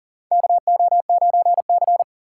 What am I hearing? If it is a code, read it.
Morse code, heard as a keyed tone, about 40 wpm, 700 Hz. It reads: KO9C
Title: USA Call Signs - 4 Characters - 2x Character Spacing - 40wpm